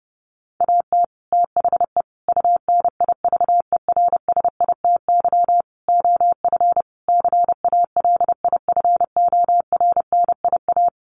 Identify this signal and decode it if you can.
Morse code, heard as a keyed tone, about 30 wpm, 700 Hz. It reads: AT THE UNIVERSITY OF CALIFORNIA
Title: Sets of 5 Words Encore2 30wpm